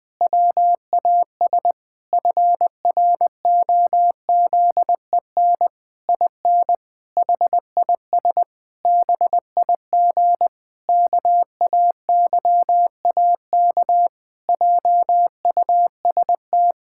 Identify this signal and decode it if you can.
Morse code, heard as a keyed tone, about 20 wpm, 700 Hz. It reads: WAS FROZEN IN HIS BIG KAYAK JUST
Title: Pangram Challenge v2 - 20wpm